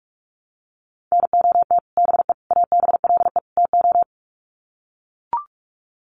Morse code, heard as a keyed tone, about 44 wpm, 700 Hz. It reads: NOT BE ABLE TO
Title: Sets of 4 Words 45wpm